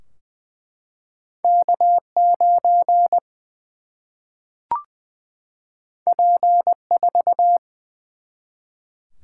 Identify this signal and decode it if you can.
Morse code, heard as a keyed tone, about 20 wpm, 700 Hz. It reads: K 9 P 4
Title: Single-Letter Single-Number 20wpm